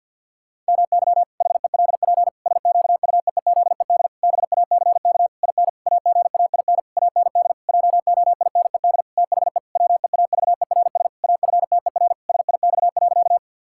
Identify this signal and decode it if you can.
Morse code, heard as a keyed tone, about 50 wpm, 700 Hz. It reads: MY HELP SQUEEZED BACK IN AGAIN AND JOINED THE WEAVERS AFTER SIX.
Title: Pangram Challenge v2 - 50wpm